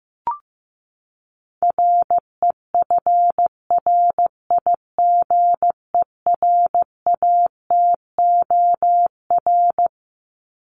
Morse code, heard as a keyed tone, about 15 wpm, 700 Hz. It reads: REFRIGERATOR
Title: Top 1501-1750 Words 15wpm